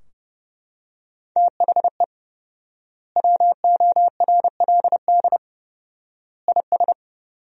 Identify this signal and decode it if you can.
Morse code, heard as a keyed tone, about 30 wpm, 700 Hz. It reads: THE WORLD IS
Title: 4x Word-Spacing - Sentences from Top 200 Words - 30wpm